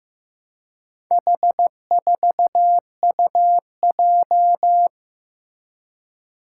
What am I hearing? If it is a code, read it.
Morse code, heard as a keyed tone, about 16 wpm, 700 Hz. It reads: H4UJ
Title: Four Challenging Letter-Number Combinations 15wpm